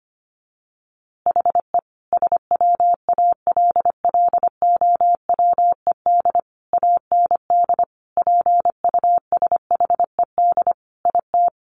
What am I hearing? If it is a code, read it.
Morse code, heard as a keyed tone, about 25 wpm, 700 Hz. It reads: HE SWALLOWED AND PUSHED IT
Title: A Sound Of Thunder 25wpm